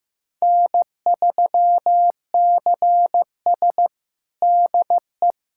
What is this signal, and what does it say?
Morse code, heard as a keyed tone, about 15 wpm, 700 Hz. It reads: N3CS DE
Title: USA Callsign DE Callsign - 4 Characters - 15wpm